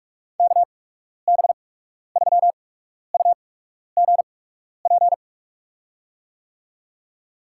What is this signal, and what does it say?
Morse code, heard as a keyed tone, about 43 wpm, spaced as for 12 wpm, 700 Hz. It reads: KB3UGP
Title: USA Call Signs - 8x Character Spacing - 45wpm